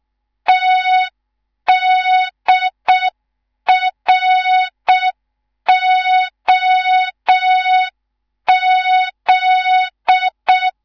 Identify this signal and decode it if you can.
Morse code, heard as a keyed tone, about 6 wpm, 3700 Hz. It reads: TDROZ
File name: Morze-HA2MN / hang / 19 GYAKORLAT